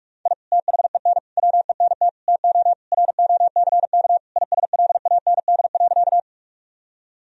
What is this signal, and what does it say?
Morse code, heard as a keyed tone, about 46 wpm, 700 Hz. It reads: I THEN WENT TO ROCK ISLAND.
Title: Sentences from Top 1000 Words - Review Part 3 - 45wpm